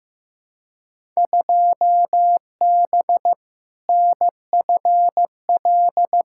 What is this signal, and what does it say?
Morse code, heard as a keyed tone, about 15 wpm, 700 Hz. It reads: 2B NFL
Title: Field Day Exchanges Encore - 15wpm